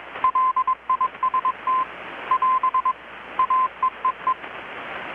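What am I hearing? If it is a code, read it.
Morse code, heard as a keyed tone, about 22 wpm, 1030 Hz. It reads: LIST <AS> AEEE
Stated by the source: Morse from KPH's Night of Nights XX maritime radiotelegraphy event, received in Chiapas, Mexico about 3,800 km away on a Kenwood R-600 with a grounded 10 m long-wire antenna.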